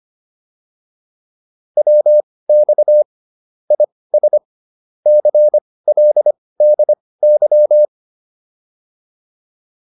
Morse code, heard as a keyed tone, about 25 wpm, spaced as for 18 wpm, 600 Hz. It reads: WX IS CLDY